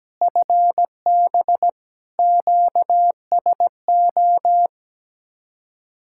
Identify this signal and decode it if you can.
Morse code, heard as a keyed tone, about 17 wpm, 700 Hz. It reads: FB QSO
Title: Speed-Racing Common QSO Phrases - 17wpm